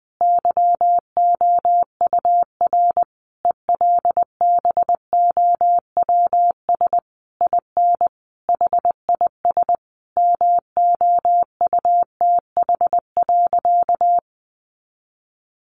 Morse code, heard as a keyed tone, about 19 wpm, 700 Hz. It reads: YOUR ELBOWS IN HIS MOUTH.
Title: A Sound Of Thunder 20wpm